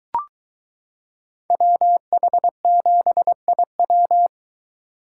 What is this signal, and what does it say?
Morse code, heard as a keyed tone, about 22 wpm, 700 Hz. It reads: WH7IW